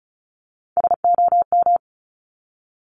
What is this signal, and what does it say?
Morse code, heard as a keyed tone, about 36 wpm, 690 Hz. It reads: SOM
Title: Top Three Letter Combinations in Words 35wpm